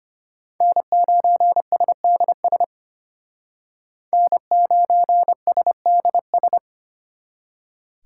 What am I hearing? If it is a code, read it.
Morse code, heard as a keyed tone, about 33 wpm, 700 Hz. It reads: N9SDS N9SDS